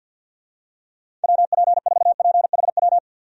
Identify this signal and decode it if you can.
Morse code, heard as a keyed tone, about 46 wpm, 700 Hz. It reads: WP4PHW
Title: USA Call Signs - 6 Characters - 50wpm